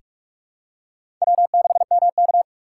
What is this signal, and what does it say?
Morse code, heard as a keyed tone, about 41 wpm, 700 Hz. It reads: W6MK
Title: USA Call Signs - 4 Characters - 2x Character Spacing - 45wpm